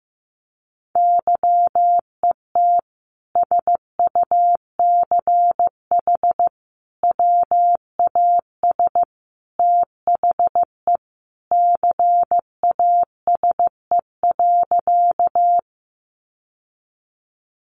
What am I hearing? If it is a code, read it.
Morse code, heard as a keyed tone, about 15 wpm, 700 Hz. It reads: YET SUCH WAS THE CASE.
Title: Sentences from Top 700 Words 15wpm